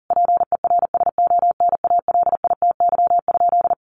Morse code, heard as a keyed tone, about 40 wpm, 700 Hz. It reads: PERSONALITY?
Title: Sentences from Top 1400 Words 40wpm